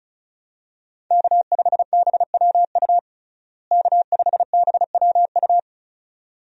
Morse code, heard as a keyed tone, about 36 wpm, 700 Hz. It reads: K5BWU K5BWU
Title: USA Call Signs - 5 Characters - Repeated 2x - 35wpm